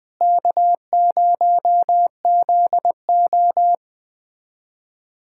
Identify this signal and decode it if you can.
Morse code, heard as a keyed tone, about 20 wpm, 700 Hz. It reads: K0ZO
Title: USA Callsign DE Callsign - 4 Characters - 20wpm